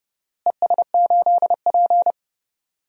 Morse code, heard as a keyed tone, about 30 wpm, 700 Hz. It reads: ES8P
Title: DXCC-100 Call Signs - 4 Characters - Repeated 3x - 30wpm